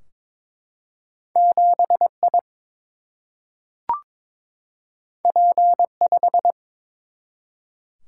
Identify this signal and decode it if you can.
Morse code, heard as a keyed tone, about 23 wpm, 700 Hz. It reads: 7 I P 5
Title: Single-Letter Single-Number 22wpm